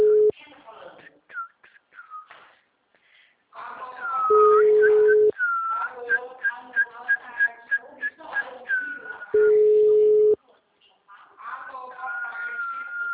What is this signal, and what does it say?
Ringback tone (Europe, 425 Hz), with speech.